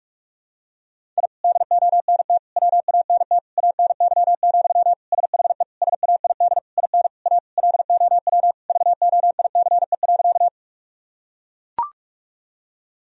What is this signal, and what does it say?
Morse code, heard as a keyed tone, about 42 wpm, 700 Hz. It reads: I DONT WANT ANY, SHE SAID IN A LOW VOICE.
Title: Sentences from Top 1000 Words - Review Part 1 - 45wpm